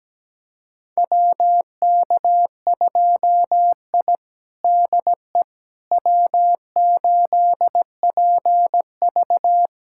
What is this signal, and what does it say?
Morse code, heard as a keyed tone, about 17 wpm, 700 Hz. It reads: WK2I DE W8PV
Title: USA Callsign DE Callsign - 4 Characters - 17wpm